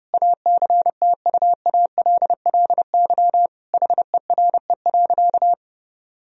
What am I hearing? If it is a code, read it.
Morse code, heard as a keyed tone, about 30 wpm, 700 Hz. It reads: ACTUALLY HERE.
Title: Sentences from Top 1000 Words - Review Part 4 - 30wpm